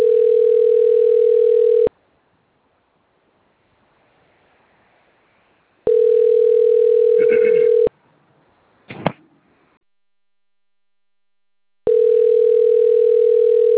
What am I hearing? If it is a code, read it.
Ringback tone (North America), 440 + 480 Hz.